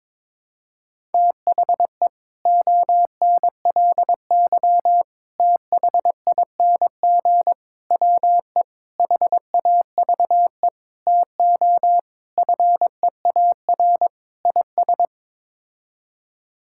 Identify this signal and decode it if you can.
Morse code, heard as a keyed tone, about 22 wpm, 700 Hz. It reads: THE ONLY THING WE HAVE TO FEAR IS
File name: practice / 22wpm / Sets of 8 Words - 22wpm